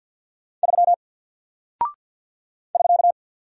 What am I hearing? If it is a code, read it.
Morse code, heard as a keyed tone, about 48 wpm, 700 Hz. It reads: 3 <SK>
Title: Single Character (Letter:Number:Punctuation:Prosign) 50wpm